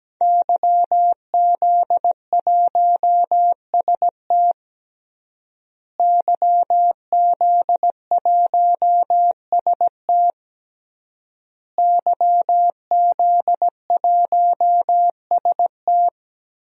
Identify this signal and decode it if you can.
Morse code, heard as a keyed tone, about 17 wpm, 700 Hz. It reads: YZ1ST YZ1ST YZ1ST
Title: DXCC-100 Call Signs - 5 Characters - Repeated 3x - 17wpm